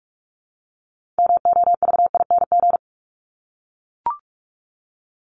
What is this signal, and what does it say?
Morse code, heard as a keyed tone, about 44 wpm, 700 Hz. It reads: MOVING
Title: ING Suffix - Top 100 Words with Base Words 45wpm